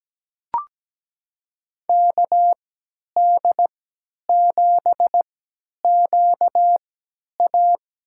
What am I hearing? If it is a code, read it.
Morse code, heard as a keyed tone, about 17 wpm, spaced as for 10 wpm, 700 Hz. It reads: KD7QA
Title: USA Call Signs - 6 Characters - 3x Character Spacing - 17wpm